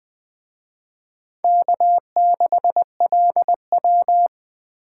Morse code, heard as a keyed tone, about 24 wpm, 700 Hz. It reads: K6LW
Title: Speed Racing - USA 1x2 Call Signs 20wpm